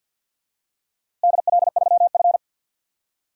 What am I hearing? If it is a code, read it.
Morse code, heard as a keyed tone, about 47 wpm, 700 Hz. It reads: DL3F
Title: DXCC-100 Call Signs - 4 Characters - Repeated 2x - 50wpm